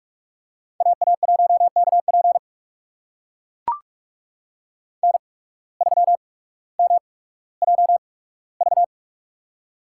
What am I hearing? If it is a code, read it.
Morse code, heard as a keyed tone, about 44 wpm, spaced as for 19 wpm, 700 Hz. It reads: AA1KP N3MJV
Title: USA Call Signs - 5 Characters - 8x Character Spacing - 45wpm